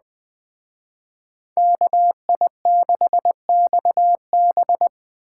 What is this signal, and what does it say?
Morse code, heard as a keyed tone, about 20 wpm, 700 Hz. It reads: KI6XB
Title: Speed Racing - USA 2x2 Call Signs 17wpm